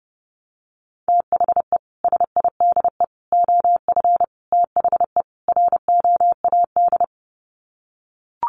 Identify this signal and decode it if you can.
Morse code, heard as a keyed tone, about 30 wpm, 700 Hz. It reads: THE SIDE OF THE ROAD
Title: Sets of 5 Words 30wpm